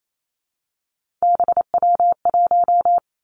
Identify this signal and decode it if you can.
Morse code, heard as a keyed tone, about 28 wpm, 700 Hz. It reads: BW1
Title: Three Challenging Letter-Number Combinations 28wpm